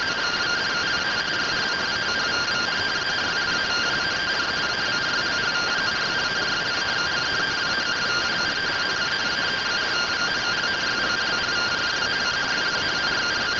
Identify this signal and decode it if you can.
Two-tone teleprinter signal, about 170 Hz shift.
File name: findings / MR AXIAL 33,99mm 1.gen-21 rtty rx